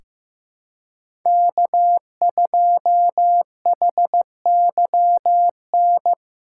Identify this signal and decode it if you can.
Morse code, heard as a keyed tone, about 15 wpm, 700 Hz. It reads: K2HYN
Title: USA Call Signs - 5 Characters - 2x Character Spacing - 15wpm